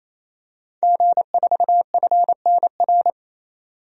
Morse code, heard as a keyed tone, about 28 wpm, 700 Hz. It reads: G4FNR
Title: DXCC-100 Call Signs - 5 Characters - 28wpm